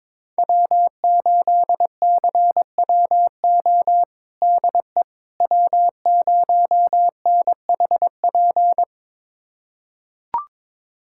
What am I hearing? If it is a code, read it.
Morse code, heard as a keyed tone, about 22 wpm, 700 Hz. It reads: W8CWO DE W0NHP
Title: USA Callsign DE Callsign - 5 Characters - 22wpm